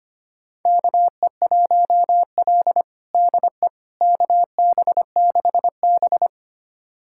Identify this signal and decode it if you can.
Morse code, heard as a keyed tone, about 25 wpm, 700 Hz. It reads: KE1L DE KB6B
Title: USA Callsign DE Callsign - 4 Characters - 25wpm